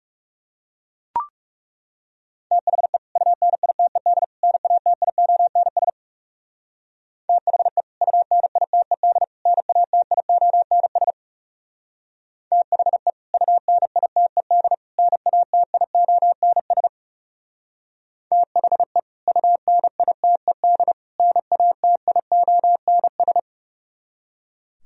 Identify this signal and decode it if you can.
Morse code, heard as a keyed tone, about 38 wpm, 700 Hz. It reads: THE UNITED NATIONS THE UNITED NATIONS THE UNITED NATIONS THE UNITED NATIONS